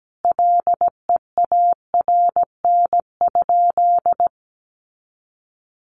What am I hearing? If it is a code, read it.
Morse code, heard as a keyed tone, about 17 wpm, 700 Hz. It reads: LEARN?
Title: Sentences from Top 300 Words 17wpm